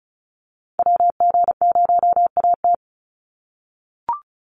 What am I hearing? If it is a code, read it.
Morse code, heard as a keyed tone, about 34 wpm, 700 Hz. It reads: WG0AT
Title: NA SOTA Chaser Callsigns 35wpm